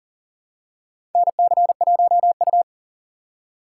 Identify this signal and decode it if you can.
Morse code, heard as a keyed tone, about 40 wpm, 700 Hz. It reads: NC1U
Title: USA Call Signs - 4 Characters - 6x Character Spacing - 40wpm